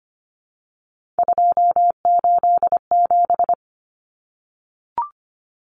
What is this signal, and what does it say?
Morse code, heard as a keyed tone, about 25 wpm, 700 Hz. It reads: 287
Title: Numbers - 3-Digit 25wpm